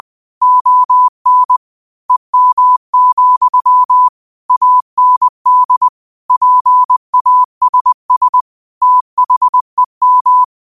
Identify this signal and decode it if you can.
Morse code, heard as a keyed tone, about 20 wpm, 1000 Hz. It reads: ON EM, AND PASS THEM
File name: qotdfiles / 20wpm